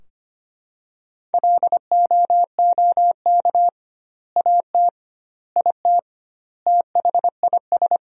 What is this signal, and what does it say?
Morse code, heard as a keyed tone, about 25 wpm, 700 Hz. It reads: LOOK AT IT THIS
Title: 2x Word-Spacing - Sentences from Top 100 Words - 25wpm